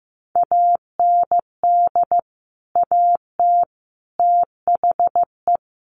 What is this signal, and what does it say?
Morse code, heard as a keyed tone, about 15 wpm, 700 Hz. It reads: AND AT THE
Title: Sets of 5 Words 15wpm